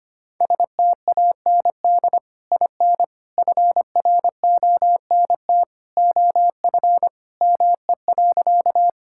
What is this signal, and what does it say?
Morse code, heard as a keyed tone, about 25 wpm, 700 Hz. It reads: STAND IN FRONT OF ME.
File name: practice / 25wpm / Sentences with 2-Word Phrases - 25wpm